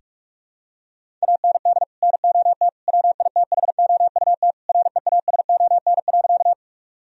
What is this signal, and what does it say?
Morse code, heard as a keyed tone, about 41 wpm, 700 Hz. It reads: AND NOT WITHOUT REASON.
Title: Sentences from Top 1000 Words - Review Part 3 - 45wpm